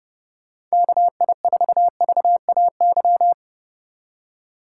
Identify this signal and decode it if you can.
Morse code, heard as a keyed tone, about 32 wpm, 700 Hz. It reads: KI4VAY
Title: USA Call Signs - 6 Characters - 2x Character Spacing - 30wpm